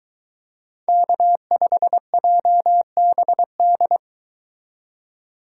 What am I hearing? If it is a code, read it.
Morse code, heard as a keyed tone, about 23 wpm, 700 Hz. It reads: K5JBD